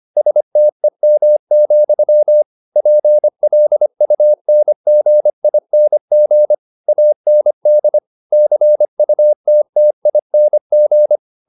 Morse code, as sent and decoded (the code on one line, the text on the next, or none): ... - . -- --..-- / .--. .-.. ..- -. --. .. -. --. / .- -. -.. / -.-. ..- - - .. -. --.
STEM, PLUNGING AND CUTTING